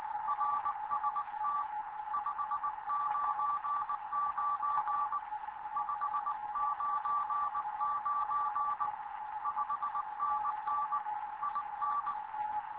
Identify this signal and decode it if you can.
Morse code, heard as a keyed tone, about 19 wpm, 1160 Hz, with radio noise. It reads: RST 599 599 5NN IN
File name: Morze-HA2MN / hangp / 2 11 GYAKORLAT